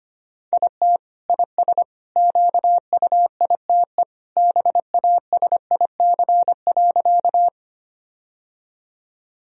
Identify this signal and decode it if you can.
Morse code, heard as a keyed tone, about 25 wpm, 700 Hz. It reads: IT IS QUITE BASIC.
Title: Sentences from Top 1000 Words - Review Part 3 - 25wpm